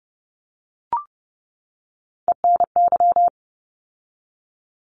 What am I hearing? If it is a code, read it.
Morse code, heard as a keyed tone, about 30 wpm, 700 Hz. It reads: ENY
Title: ARRL Sections 30wpm